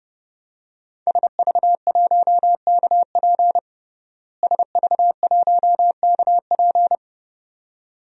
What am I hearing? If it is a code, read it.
Morse code, heard as a keyed tone, about 30 wpm, 700 Hz. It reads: SV1KP SV1KP
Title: DXCC-100 Call Signs - 5 Characters - Repeated 2x - 30wpm